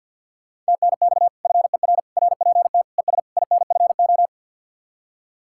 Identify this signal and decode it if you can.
Morse code, heard as a keyed tone, about 63 wpm, 700 Hz. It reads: TNX FER RPT ES INFO